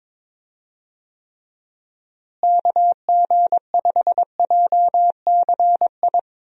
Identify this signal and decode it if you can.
Morse code, heard as a keyed tone, about 22 wpm, 700 Hz. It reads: KG5JCI
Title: USA Call Signs - 6 Characters - Repeated 3x - 22wpm